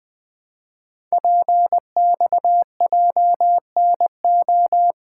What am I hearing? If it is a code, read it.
Morse code, heard as a keyed tone, about 20 wpm, 700 Hz. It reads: PXJNO
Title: Code Groups - 5 Letters - 20wpm